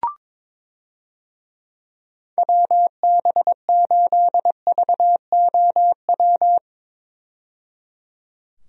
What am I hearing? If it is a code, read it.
Morse code, heard as a keyed tone, about 22 wpm, 700 Hz. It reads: WB8VOW
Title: USA Call Signs - 6 Characters - 22wpm